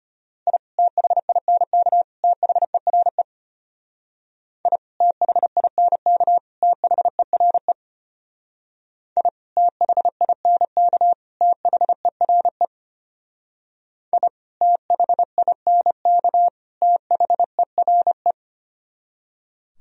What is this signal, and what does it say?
Morse code, heard as a keyed tone, about 32 wpm, 700 Hz. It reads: I THINK THERE I THINK THERE I THINK THERE I THINK THERE